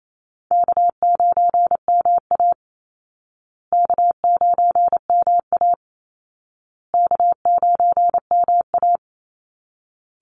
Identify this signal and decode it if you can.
Morse code, heard as a keyed tone, about 28 wpm, 700 Hz. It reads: K9MA K9MA K9MA
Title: Call Signs Repeated 2x - 28wpm